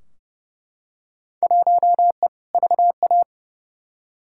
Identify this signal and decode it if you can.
Morse code, heard as a keyed tone, about 30 wpm, 700 Hz. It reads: 1E VA
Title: Field Day Exchanges - 30wpm